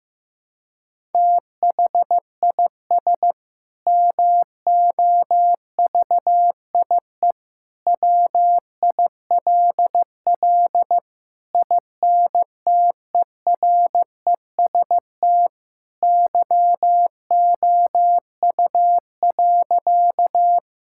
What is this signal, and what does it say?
Morse code, heard as a keyed tone, about 15 wpm, 700 Hz. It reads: THIS MOVIE WILL INTEREST YOU.
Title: Sentences from Top 1000 Words - Review Part 5 - 15wpm